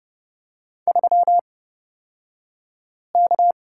Morse code, heard as a keyed tone, about 30 wpm, spaced as for 5 wpm, 700 Hz. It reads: 3K
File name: Lesson 36d - BT - Two Characters 30wpm